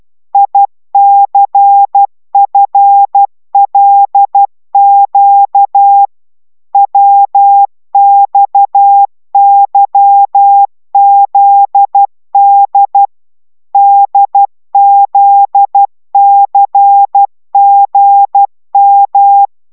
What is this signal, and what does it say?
Morse code, heard as a keyed tone, about 12 wpm, 800 Hz. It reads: ICFLQ WXYZD DZCGM
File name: taviro tanulas / cw 69 19 10